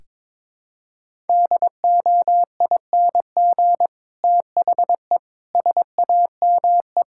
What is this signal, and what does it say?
Morse code, heard as a keyed tone, about 21 wpm, 700 Hz. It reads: DOING THE SAME
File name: practice / 22wpm / Sets of 8 Words - 22wpm